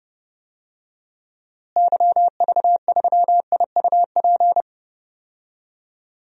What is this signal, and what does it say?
Morse code, heard as a keyed tone, about 32 wpm, 700 Hz. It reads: YV3IUP
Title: DXCC-100 Call Signs - 6 Characters - Repeated 3x - 30wpm